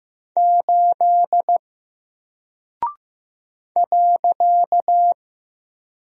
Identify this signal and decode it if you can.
Morse code, heard as a keyed tone, about 15 wpm, 700 Hz. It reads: 8 .
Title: Single Character (Letter:Number:Punctuation:Prosign) 15wpm